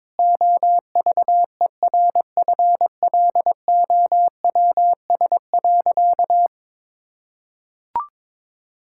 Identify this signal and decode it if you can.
Morse code, heard as a keyed tone, about 22 wpm, 700 Hz. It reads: OVERFLOWS.
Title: Pangram Challenge v2 - 22wpm